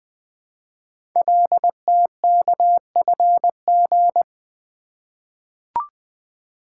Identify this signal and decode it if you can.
Morse code, heard as a keyed tone, about 20 wpm, 700 Hz. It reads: LTKFG
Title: Code Groups - 5 Letters - 20wpm